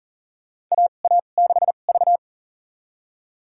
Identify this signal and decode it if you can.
Morse code, heard as a keyed tone, about 40 wpm, spaced as for 29 wpm, 690 Hz. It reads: AA6V
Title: USA Call Signs - 4 Characters - 2x Character Spacing - 40wpm